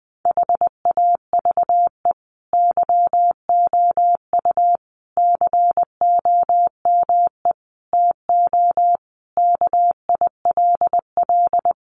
Morse code, heard as a keyed tone, about 20 wpm, 700 Hz. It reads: HAVE YOU COME TO KILL
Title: Sentences from Top 1400 Words 20wpm